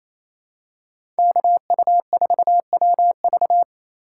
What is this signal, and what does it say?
Morse code, heard as a keyed tone, about 28 wpm, 700 Hz. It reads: KU4WV
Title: USA Call Signs - 5 Characters - 2x Character Spacing - 28wpm